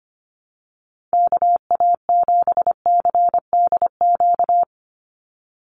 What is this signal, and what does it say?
Morse code, heard as a keyed tone, about 25 wpm, 700 Hz. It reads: KA7CDQ
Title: USA Call Signs 25wpm